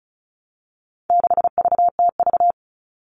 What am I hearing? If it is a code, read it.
Morse code, heard as a keyed tone, about 34 wpm, 700 Hz. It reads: 6VTV
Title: All Cut Numbers - 4-Digit 35wpm